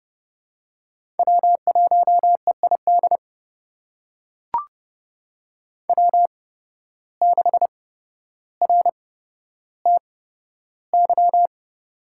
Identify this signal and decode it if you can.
Morse code, heard as a keyed tone, about 32 wpm, spaced as for 13 wpm, 700 Hz. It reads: W1EID W6RTY
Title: USA Call Signs - 5 Characters - 8x Character Spacing - 30wpm